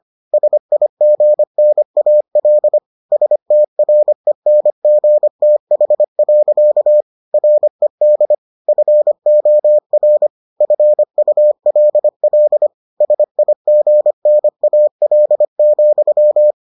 Morse code, heard as a keyed tone, about 25 wpm, 600 Hz. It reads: SIGNAL STRENGTH. RED FOR FULL SIGNAL,